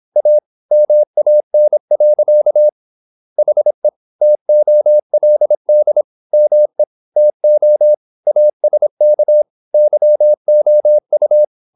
Morse code, heard as a keyed tone, about 26 wpm, 600 Hz. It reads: A MAN. HE TOLD ME TO ASK YOU
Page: Never 26wpm | CW med Gnister